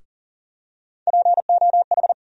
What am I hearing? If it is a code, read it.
Morse code, heard as a keyed tone, about 40 wpm, 700 Hz. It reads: POH
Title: USA Call Sign Postfixes - 3 Letters 40wpm